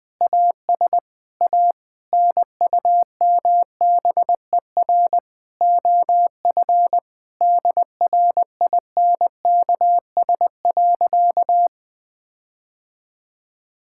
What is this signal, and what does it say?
Morse code, heard as a keyed tone, about 20 wpm, 700 Hz. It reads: AS A NUMBER OF DRINKS.
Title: Sentences with 6-Word Phrases - Spoken Beginning - 20wpm